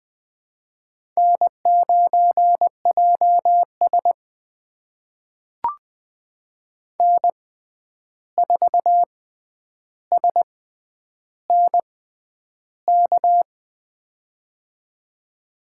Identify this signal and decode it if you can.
Morse code, heard as a keyed tone, about 20 wpm, spaced as for 10 wpm, 700 Hz. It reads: N9JS N4SNK
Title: USA Call Signs - 6x Character Spacing - 20wpm